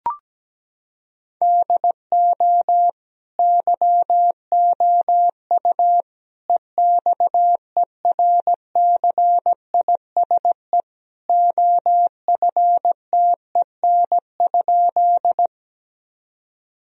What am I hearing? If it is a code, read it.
Morse code, heard as a keyed tone, about 17 wpm, 700 Hz. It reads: DO YOU EXERCISE OFTEN?
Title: Sentences from Top 1000 Words - Review Part 4 - 17wpm